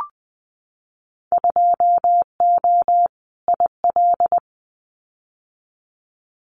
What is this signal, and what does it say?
Morse code, heard as a keyed tone, about 20 wpm, 700 Hz. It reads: 2O IL
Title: Winter Field Day Exchanges 20wpm